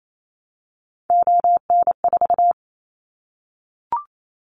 Morse code, heard as a keyed tone, about 28 wpm, 700 Hz. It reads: ON4
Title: Callsign Prefixes for Top 25 DXCC - 28wpm